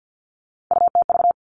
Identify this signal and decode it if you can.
Morse code, heard as a keyed tone, about 50 wpm, 700 Hz. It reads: UTV
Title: Three Challenging Letter-Number Combinations 50wpm